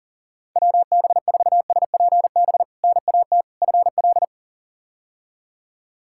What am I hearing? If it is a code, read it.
Morse code, heard as a keyed tone, about 40 wpm, 700 Hz. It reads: WB4SPB NAT FL
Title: K1USN SST QSO Practice - 40wpm